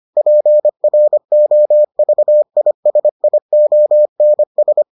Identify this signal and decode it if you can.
Morse code, heard as a keyed tone, about 25 wpm, 600 Hz. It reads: PROVISIONS